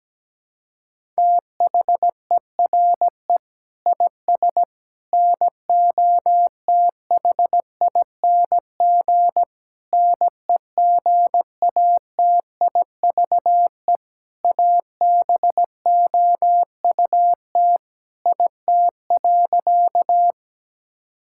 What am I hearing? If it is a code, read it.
Morse code, heard as a keyed tone, about 17 wpm, 700 Hz. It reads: THERE IS NOTHING NEGATIVE ABOUT IT.
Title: Sentences from Top 1000 Words - Review Part 6 - 17wpm